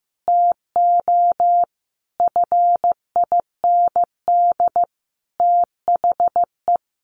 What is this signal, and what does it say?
Morse code, heard as a keyed tone, about 15 wpm, 700 Hz. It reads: TO FIND THE
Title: Sentences from Top 1200 Words 15wpm